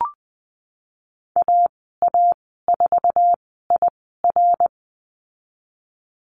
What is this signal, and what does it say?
Morse code, heard as a keyed tone, about 20 wpm, spaced as for 15 wpm, 700 Hz. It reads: AA4IR
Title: 076 Lesson 13g - Letter L - Call Signs 20wpm